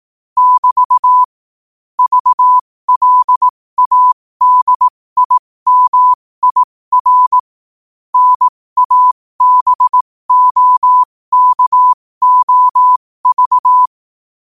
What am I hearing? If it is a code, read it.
Morse code, heard as a keyed tone, about 18 wpm, spaced as for 15 wpm, 1000 Hz. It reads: <BT> VLADIMIR NABOKOV
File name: qotdfiles / 15wpm